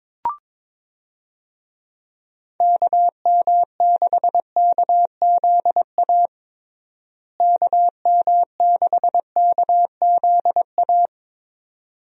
Morse code, heard as a keyed tone, about 22 wpm, 700 Hz. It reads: KM6KZA KM6KZA
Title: USA Call Signs - 6 Characters - Repeated 2x - 22wpm